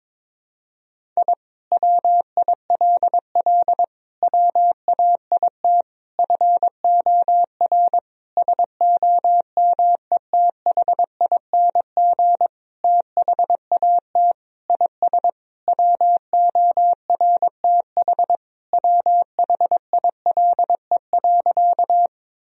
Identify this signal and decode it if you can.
Morse code, heard as a keyed tone, about 22 wpm, 700 Hz. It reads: I WILL WAIT FOR SOMETHING THAT IS WORTH WHILE.
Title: Sentences from Top 1000 Words - Review Part 1 - 22wpm